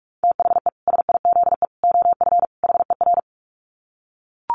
Morse code, heard as a keyed tone, about 44 wpm, 700 Hz. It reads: THE SIZE OF HER
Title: Dit Challenge 45wpm